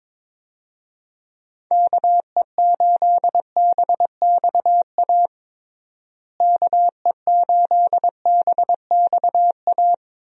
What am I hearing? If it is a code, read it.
Morse code, heard as a keyed tone, about 21 wpm, 700 Hz. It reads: KE8BXA KE8BXA
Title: USA Call Signs - 6 Characters - Repeated 3x - 22wpm